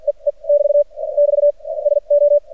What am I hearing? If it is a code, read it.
Morse code, heard as a keyed tone, about 50 wpm, spaced as for 39 wpm, 580 Hz. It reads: E E <BT> <BT> SO
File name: archived / CRSnet